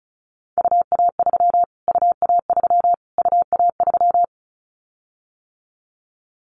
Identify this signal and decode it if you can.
Morse code, heard as a keyed tone, about 34 wpm, 700 Hz. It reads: UA3 UA3 UA3
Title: Callsign Prefixes for Top 25 DXCC - Repeated 3x - 35wpm